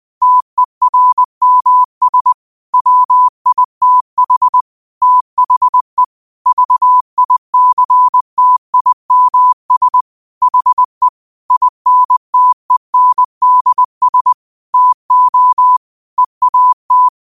Morse code, heard as a keyed tone, about 20 wpm, 1000 Hz. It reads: TERMS WITH THE VICTIMS HE INTENDS TO EAT
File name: qotdfiles / 20wpm